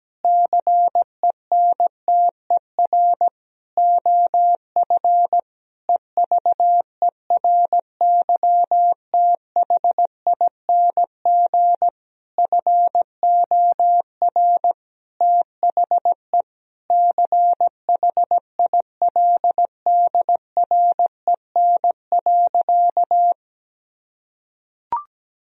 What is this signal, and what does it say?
Morse code, heard as a keyed tone, about 17 wpm, 700 Hz. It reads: CENTER OF EVERYTHING FOR THE CHILDREN.
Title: Sentences from Top 1000 Words - Review Part 2 - 17wpm